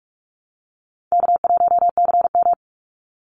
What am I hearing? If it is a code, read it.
Morse code, heard as a keyed tone, about 44 wpm, 700 Hz. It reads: K1CM
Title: NA SOTA Chaser Callsigns 45wpm